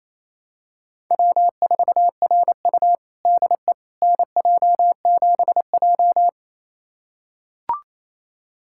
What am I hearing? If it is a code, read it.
Morse code, heard as a keyed tone, about 28 wpm, 700 Hz. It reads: W4RU DE NJ7J
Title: USA Callsign DE Callsign - 4 Characters - 28wpm